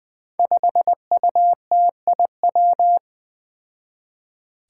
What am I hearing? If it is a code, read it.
Morse code, heard as a keyed tone, about 20 wpm, 700 Hz. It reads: 5UTIW
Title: Five Challenging Letter-Number Combinations 20wpm